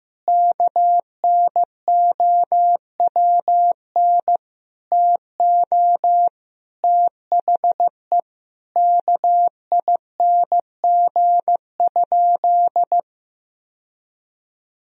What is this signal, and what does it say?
Morse code, heard as a keyed tone, about 16 wpm, 700 Hz. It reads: KNOWN TO THE KING?
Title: Sentences from Top 1000 Words - Review Part 4 - 15wpm